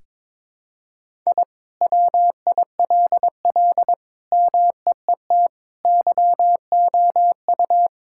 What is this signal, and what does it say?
Morse code, heard as a keyed tone, about 22 wpm, 700 Hz. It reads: I WILL MEET YOU
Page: morse-code-ninja-practice-22wpm